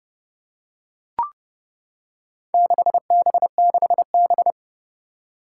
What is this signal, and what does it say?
Morse code, heard as a keyed tone, about 30 wpm, 700 Hz. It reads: 6B6B
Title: Four Challenging Letter-Number Combinations 30wpm